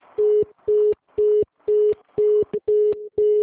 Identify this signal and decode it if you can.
Congestion tone (425 Hz).
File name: out- -2001- - - .414